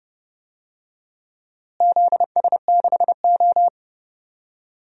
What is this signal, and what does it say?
Morse code, heard as a keyed tone, about 32 wpm, 700 Hz. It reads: ZS6O
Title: DXCC-100 Call Signs - 4 Characters - Repeated 3x - 30wpm